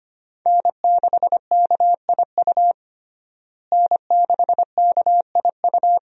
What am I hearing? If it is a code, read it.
Morse code, heard as a keyed tone, about 25 wpm, 700 Hz. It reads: N6KIU N6KIU
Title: USA Call Signs - 5 Characters - Repeated 3x - 25wpm